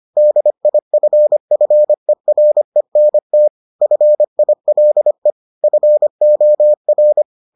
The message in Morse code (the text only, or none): DIFFERENT FILE FOR